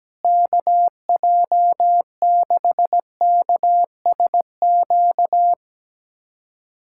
Morse code, heard as a keyed tone, about 17 wpm, 700 Hz. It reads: KJ6KSQ
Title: USA Callsign DE Callsign 17wpm